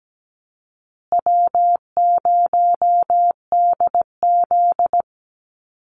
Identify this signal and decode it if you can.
Morse code, heard as a keyed tone, about 17 wpm, 700 Hz. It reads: W0DZ
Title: USA Call Signs Repeated 2x 17wpm